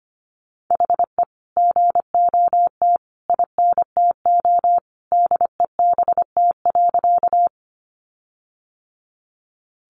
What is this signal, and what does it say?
Morse code, heard as a keyed tone, about 25 wpm, 700 Hz. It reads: HE GOT INTO DEBT.
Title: Sentences from Top 1300 Words 25wpm